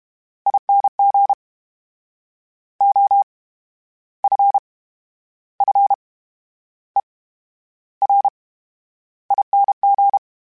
Morse code, heard as a keyed tone, about 32 wpm, 800 Hz. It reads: ING O F F E R ING
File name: Special INGs / Special INGs-mp3s / VerySlowToFaster Ing @30